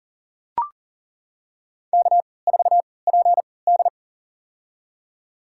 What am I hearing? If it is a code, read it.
Morse code, heard as a keyed tone, about 39 wpm, spaced as for 22 wpm, 700 Hz. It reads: K4PD
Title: USA Call Signs - 4 Characters - 3x Character Spacing - 40wpm